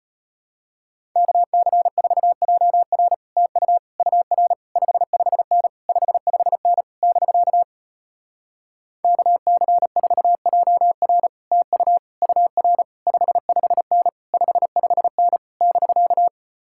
Morse code, heard as a keyed tone, about 33 wpm, 700 Hz. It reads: KC4JR TU UR 55N 55N <BK> KC4JR TU UR 55N 55N <BK>